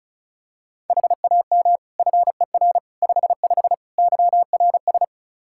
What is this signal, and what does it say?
Morse code, heard as a keyed tone, about 36 wpm, 700 Hz. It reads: HAM FER 55 YRS
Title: Practice QSO - Exchange 3 - More Info - 35wpm